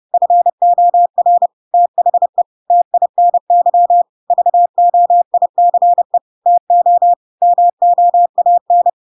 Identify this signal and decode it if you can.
Morse code, heard as a keyed tone, about 30 wpm, 700 Hz. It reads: FOR THE TINY VOICE TO MOAN
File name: Pinocchio / Pinocchio ch1 30 30 ik1hge